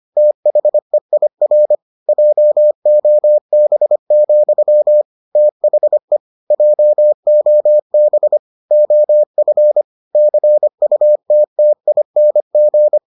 Morse code, heard as a keyed tone, about 25 wpm, 600 Hz. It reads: THEIR JOB, THE JOB OF CUTTING